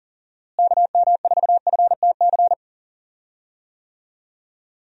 Morse code, heard as a keyed tone, about 40 wpm, 700 Hz. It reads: KM4FTC
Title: USA Call Signs - 6 Characters - Repeated 3x - 40wpm